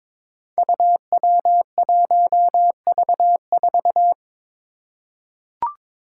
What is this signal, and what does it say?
Morse code, heard as a keyed tone, about 21 wpm, 700 Hz. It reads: UW1V4
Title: Five Challenging Letter-Number Combinations 22wpm